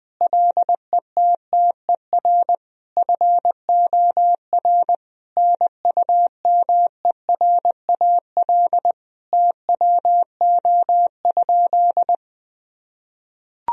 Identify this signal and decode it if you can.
Morse code, heard as a keyed tone, about 20 wpm, 700 Hz. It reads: LETTER FOR NUMERAL TWO?
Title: Sentences from Top 1000 Words - Review Part 2 - 20wpm